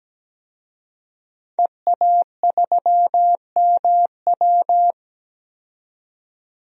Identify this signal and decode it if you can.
Morse code, heard as a keyed tone, about 17 wpm, 700 Hz. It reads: EA3MW
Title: DXCC-100 Call Signs - 5 Characters - 17wpm